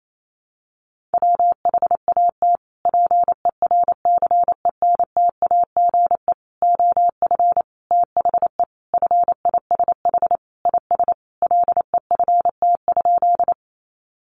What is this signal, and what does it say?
Morse code, heard as a keyed tone, about 28 wpm, 700 Hz. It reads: WHAT PERCENTAGE OF THE FISH IS LEFT?
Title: Sentences from Top 1300 Words 28wpm